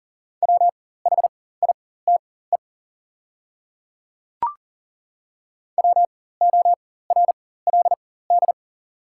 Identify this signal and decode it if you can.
Morse code, heard as a keyed tone, about 37 wpm, spaced as for 18 wpm, 690 Hz. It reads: WHITE WORLD
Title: 4x Character Spacing - Top 500 Words - 40wpm